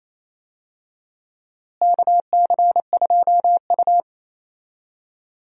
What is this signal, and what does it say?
Morse code, heard as a keyed tone, about 28 wpm, 700 Hz. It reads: KC2U
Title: USA Call Signs - 4 Characters - Repeated 2x - 28wpm